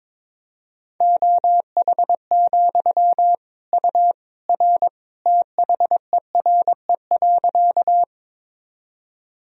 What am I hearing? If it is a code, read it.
Morse code, heard as a keyed tone, about 22 wpm, 700 Hz. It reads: OH, U R THERE.
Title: Sentences from Top 500 Words and Abbreviations 22wpm